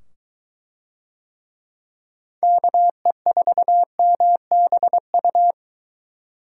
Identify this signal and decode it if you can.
Morse code, heard as a keyed tone, about 22 wpm, 700 Hz. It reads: KE4MBU